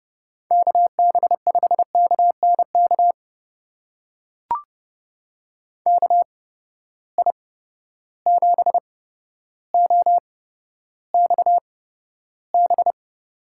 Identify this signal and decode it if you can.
Morse code, heard as a keyed tone, about 32 wpm, spaced as for 13 wpm, 700 Hz. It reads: KB5KNK KI7OXB
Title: USA Call Signs - 6 Characters - 8x Character Spacing - 30wpm